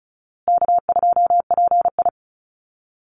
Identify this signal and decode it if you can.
Morse code, heard as a keyed tone, about 34 wpm, 700 Hz. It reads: K2PI
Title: Easy Call Signs Repeated 3x - 35wpm